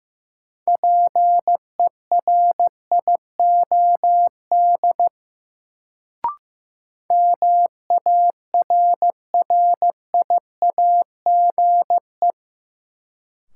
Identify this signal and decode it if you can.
Morse code, heard as a keyed tone, about 15 wpm, 700 Hz. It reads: PERIOD MARRIAGE
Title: Sentences from Top 800 Words - New Word Drill - 15wpm